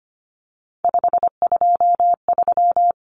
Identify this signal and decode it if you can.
Morse code, heard as a keyed tone, about 25 wpm, 700 Hz. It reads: 523
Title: Numbers - 3-Digit 25wpm